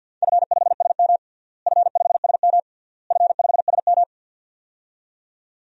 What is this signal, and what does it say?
Morse code, heard as a keyed tone, about 53 wpm, 700 Hz. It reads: F5SM F5SM F5SM
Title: DXCC-100 Call Signs - 4 Characters - Repeated 3x - 50wpm